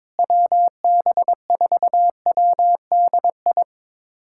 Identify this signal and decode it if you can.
Morse code, heard as a keyed tone, about 22 wpm, 700 Hz. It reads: WB4WDI
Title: USA Callsign DE Callsign - 6 Characters - 22wpm